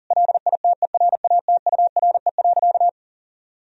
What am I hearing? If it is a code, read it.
Morse code, heard as a keyed tone, about 40 wpm, 700 Hz. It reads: LITERATURE.
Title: Sentences from Top 1000 Words - Review Part 3 - 40wpm